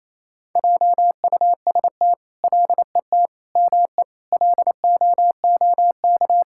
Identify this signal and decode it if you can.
Morse code, heard as a keyed tone, about 28 wpm, 700 Hz. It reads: JUST LET ME LOOK
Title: Sentences from Top 1000 Words - Review Part 5 - 28wpm